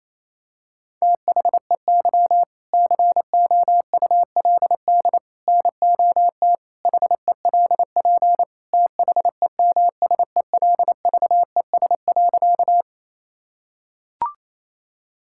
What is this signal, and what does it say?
Morse code, heard as a keyed tone, about 27 wpm, 700 Hz. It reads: THEY COULD NOT HELP THEMSELVES.
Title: Sentences from Top 1000 Words - Review Part 5 - 28wpm